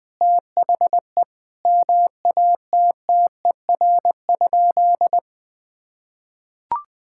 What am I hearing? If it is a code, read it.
Morse code, heard as a keyed tone, about 21 wpm, 700 Hz. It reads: THE MATTER?
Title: Sentences with 6-Word Phrases - Spoken Beginning - 20wpm